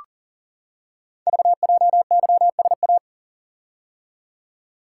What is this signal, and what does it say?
Morse code, heard as a keyed tone, about 42 wpm, 700 Hz. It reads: VJYSA
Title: Code Groups - 5 Letters - 40wpm